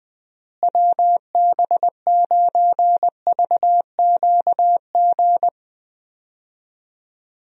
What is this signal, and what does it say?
Morse code, heard as a keyed tone, about 20 wpm, 700 Hz. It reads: WB9VQG
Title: USA Call Signs - 6 Characters - Repeated 2x - 20wpm